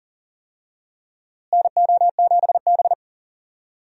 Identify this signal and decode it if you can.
Morse code, heard as a keyed tone, about 40 wpm, 700 Hz. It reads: NO7B
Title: USA Call Signs - 4 Characters - Repeated 2x - 40wpm